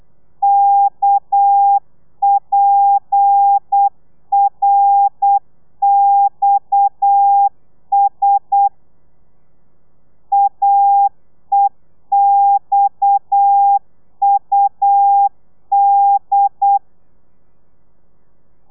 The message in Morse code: KPRXS AEXUD